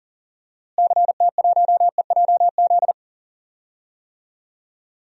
Morse code, heard as a keyed tone, about 40 wpm, 700 Hz. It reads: CT1EJZ
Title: DXCC-100 Call Signs - 6 Characters - Repeated 2x - 40wpm